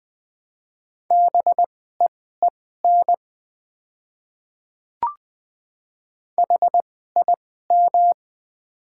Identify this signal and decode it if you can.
Morse code, heard as a keyed tone, about 20 wpm, spaced as for 15 wpm, 700 Hz. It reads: BEEN HIM
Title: 2x Character Spacing - Top 100 Words - 20wpm